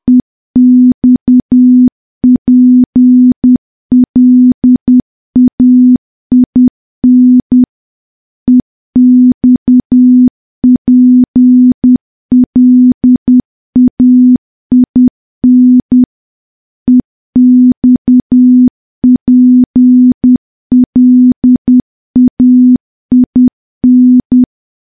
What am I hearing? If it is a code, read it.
Morse code, heard as a keyed tone, about 10 wpm, 250 Hz. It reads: EXPLAIN EXPLAIN EXPLAIN
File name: morse / morse-training-10WPM